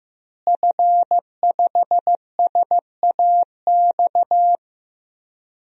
Morse code, heard as a keyed tone, about 16 wpm, 700 Hz. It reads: F5SAX
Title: DXCC-100 Call Signs - 5 Characters - Repeated 3x - 15wpm